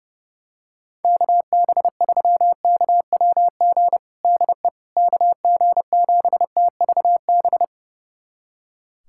Morse code, heard as a keyed tone, about 30 wpm, 700 Hz. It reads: KB3KWG DE KG7TVB
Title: USA Callsign DE Callsign 30wpm